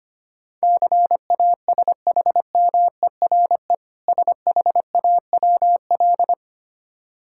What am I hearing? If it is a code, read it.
Morse code, heard as a keyed tone, about 25 wpm, 690 Hz. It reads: CASHMERE SHAWL
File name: practice / 25wpm / Difficult Words with Hints - 25wpm